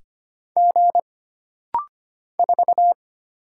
Morse code, heard as a keyed tone, about 25 wpm, 700 Hz. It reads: G 4
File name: Single Character (Letter:Number:Punctuation:Prosign) - Rapid-Fire - 25wpm